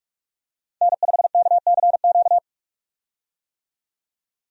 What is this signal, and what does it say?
Morse code, heard as a keyed tone, about 45 wpm, 700 Hz. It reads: N5KCQ
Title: USA Call Signs - 5 Characters - 45wpm